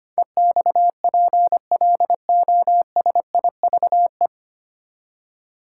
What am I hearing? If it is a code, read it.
Morse code, heard as a keyed tone, about 25 wpm, 690 Hz. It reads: EXPLOSIVE
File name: practice / 25wpm / 9 Letter Words - First 3 Letters Given - 25wpm